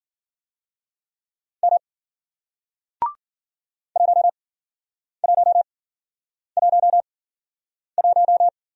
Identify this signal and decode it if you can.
Morse code, heard as a keyed tone, about 54 wpm, spaced as for 9 wpm, 700 Hz. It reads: M 1111